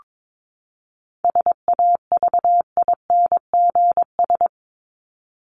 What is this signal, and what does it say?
Morse code, heard as a keyed tone, about 22 wpm, 700 Hz. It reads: SAVINGS
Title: Top 1001-1250 Words 22wpm